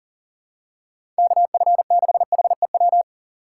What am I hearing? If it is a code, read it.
Morse code, heard as a keyed tone, about 40 wpm, 700 Hz. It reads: KF6HEW
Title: USA Call Signs - 6 Characters - 6x Character Spacing - 40wpm